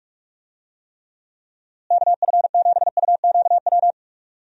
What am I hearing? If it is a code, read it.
Morse code, heard as a keyed tone, about 46 wpm, 700 Hz. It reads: KF7UQW
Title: USA Call Signs - 6 Characters - Repeated 2x - 45wpm